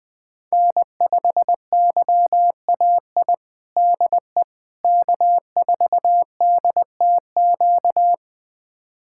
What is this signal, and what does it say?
Morse code, heard as a keyed tone, about 20 wpm, 700 Hz. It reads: N5YAI DE K4DTQ
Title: USA Callsign DE Callsign 20wpm